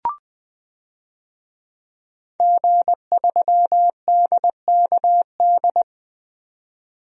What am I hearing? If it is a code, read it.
Morse code, heard as a keyed tone, about 20 wpm, 700 Hz. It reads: G3DKD
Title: DXCC-100 Call Signs - 5 Characters - 20wpm